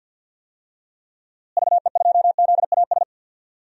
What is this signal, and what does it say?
Morse code, heard as a keyed tone, about 47 wpm, 700 Hz. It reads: VE2ZAS
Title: DXCC-100 Call Signs - 6 Characters - Repeated 2x - 50wpm